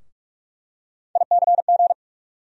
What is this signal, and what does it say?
Morse code, heard as a keyed tone, about 46 wpm, 700 Hz. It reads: ICG
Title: USA Call Sign Postfixes - 3 Letters 45wpm